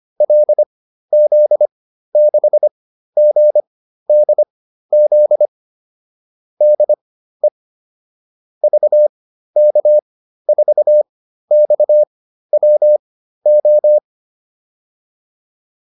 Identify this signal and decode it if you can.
Morse code, heard as a keyed tone, about 25 wpm, spaced as for 13 wpm, 600 Hz. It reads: LZ6GDZ DE VK4XWO
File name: QSO 407 13